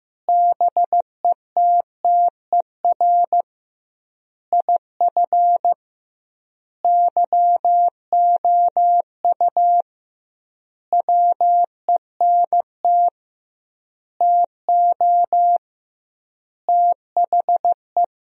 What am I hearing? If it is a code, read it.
Morse code, heard as a keyed tone, about 15 wpm, 700 Hz. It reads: BETTER IF YOU WENT TO THE
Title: 2x Word-Spacing - Sentences from Top 400 Words - 15wpm